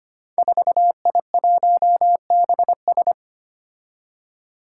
Morse code, heard as a keyed tone, about 25 wpm, 700 Hz. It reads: 4I1BS
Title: Five Challenging Letter-Number Combinations 25wpm